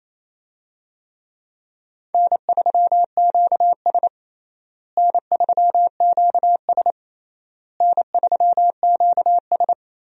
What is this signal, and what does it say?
Morse code, heard as a keyed tone, about 27 wpm, 700 Hz. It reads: N3QS N3QS N3QS
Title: USA Call Signs - 4 Characters - Repeated 3x - 28wpm